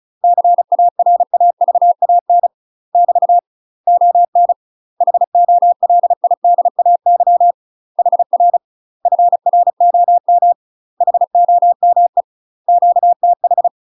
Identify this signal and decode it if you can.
Morse code, heard as a keyed tone, about 35 wpm, 700 Hz. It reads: CARAVAN <BT> ON HOLIDAY HR FROM HOME QTH
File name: UK MASTER QSOs TRAINING / 35WPM / UK MASTER QSOs 0014